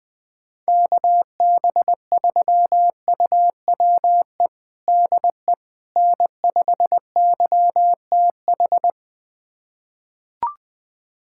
Morse code, heard as a keyed tone, about 21 wpm, 700 Hz. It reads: KB3UWE DE N5YTH
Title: USA Callsign DE Callsign 20wpm